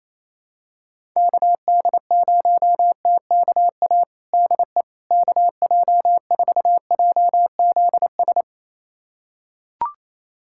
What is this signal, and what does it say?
Morse code, heard as a keyed tone, about 27 wpm, 700 Hz. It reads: KD0TKA DE KJ4JZS
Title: USA Callsign DE Callsign - 6 Characters - 28wpm